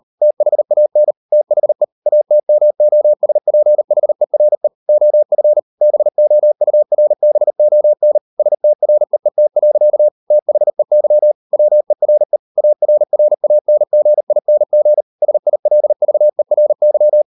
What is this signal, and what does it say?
Morse code, heard as a keyed tone, about 38 wpm, 600 Hz. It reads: THAN THE ATMOSPHERE OF BOURBON STREET. THEY WERE ARRANGING SILVERY